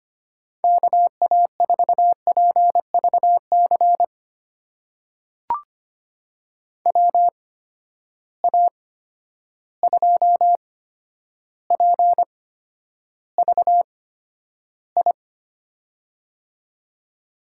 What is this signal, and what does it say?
Morse code, heard as a keyed tone, about 25 wpm, spaced as for 11 wpm, 700 Hz. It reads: KA4PVC WA2PVI
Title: USA Call Signs - 6 Characters - 8x Character Spacing - 25wpm